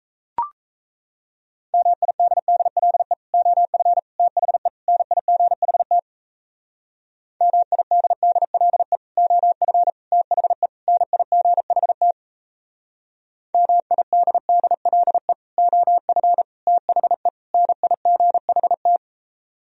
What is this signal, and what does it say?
Morse code, heard as a keyed tone, about 37 wpm, 700 Hz. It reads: MIDDLE OF THE NIGHT MIDDLE OF THE NIGHT MIDDLE OF THE NIGHT